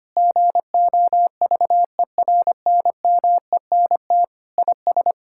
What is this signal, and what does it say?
Morse code, heard as a keyed tone, about 25 wpm, 700 Hz. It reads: GOVERNMENT IS
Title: Sentences from Top 1600 Words 25wpm